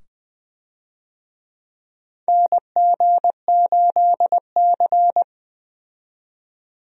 Morse code, heard as a keyed tone, about 20 wpm, 700 Hz. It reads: NG8C
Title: USA Call Signs - 4 Characters - 20wpm